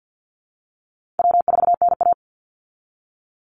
Morse code, heard as a keyed tone, about 50 wpm, 700 Hz. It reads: W4NA
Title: Easy Call Signs Repeated 2x - 50wpm